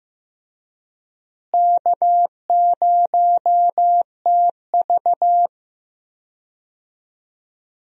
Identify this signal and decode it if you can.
Morse code, heard as a keyed tone, about 15 wpm, 700 Hz. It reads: K0TV
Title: USA Call Signs - 4 Characters - Repeated 3x - 15wpm